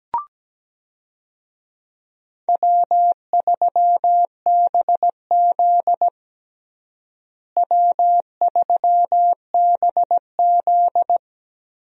Morse code, heard as a keyed tone, about 17 wpm, 700 Hz. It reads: W3BZ W3BZ
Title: USA Call Signs - 4 Characters - Repeated 3x - 17wpm